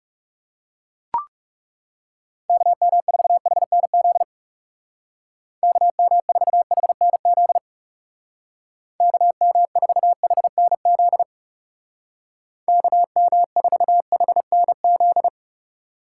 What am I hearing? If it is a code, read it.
Morse code, heard as a keyed tone, about 36 wpm, 700 Hz. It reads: KM4HNZ KM4HNZ KM4HNZ KM4HNZ
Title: Speed Racing - USA Call Signs 30wpm